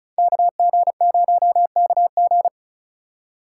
Morse code, heard as a keyed tone, about 45 wpm, 700 Hz. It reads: KG0KG